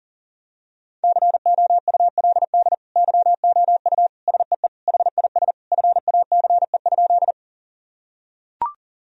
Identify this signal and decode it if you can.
Morse code, heard as a keyed tone, about 37 wpm, 700 Hz. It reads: COULD YOU SEE HIS FACE?
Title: Sentences from Top 1000 Words - Review Part 2 - 40wpm